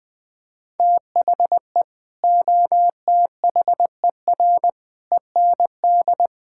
Morse code, heard as a keyed tone, about 20 wpm, 700 Hz. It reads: THE OTHER END
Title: Build A Phrase - 5-Word Phrases - 20wpm